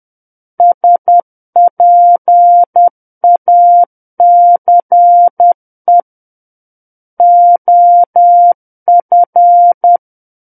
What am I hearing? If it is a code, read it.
Morse code, heard as a keyed tone, about 10 wpm, 700 Hz. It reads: SPACE OF
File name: Book One / 01-Chapter 1 - 10 WPM